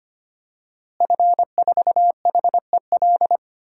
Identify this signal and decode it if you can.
Morse code, heard as a keyed tone, about 25 wpm, 700 Hz. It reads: F4HEL
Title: DXCC-100 Call Signs - 5 Characters - Repeated 3x - 25wpm